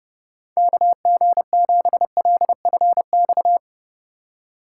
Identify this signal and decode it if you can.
Morse code, heard as a keyed tone, about 30 wpm, 700 Hz. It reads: KG7LFX